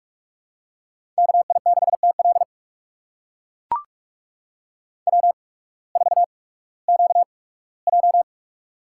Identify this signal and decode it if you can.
Morse code, heard as a keyed tone, about 44 wpm, spaced as for 21 wpm, 700 Hz. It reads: KI6TL W4QJ
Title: USA Call Signs - 8x Character Spacing - 45wpm